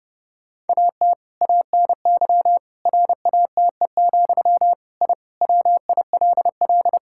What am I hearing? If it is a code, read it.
Morse code, heard as a keyed tone, about 30 wpm, 700 Hz. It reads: AT ANY RATE, I WILL
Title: Sentences from Top 1000 Words - Review Part 6 - 30wpm